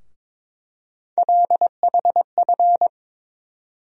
Morse code, heard as a keyed tone, about 22 wpm, 700 Hz. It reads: LHF
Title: USA Call Sign Postfixes - 3 Letters 22wpm